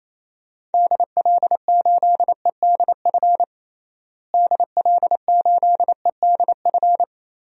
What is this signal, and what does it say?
Morse code, heard as a keyed tone, about 28 wpm, 700 Hz. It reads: DL8EDF DL8EDF
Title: DXCC-100 Call Signs - 6 Characters - Repeated 3x - 28wpm